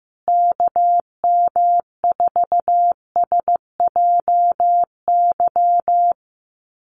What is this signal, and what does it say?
Morse code, heard as a keyed tone, about 15 wpm, 700 Hz. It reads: KM4SJY
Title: USA Call Signs Repeated 2x 15wpm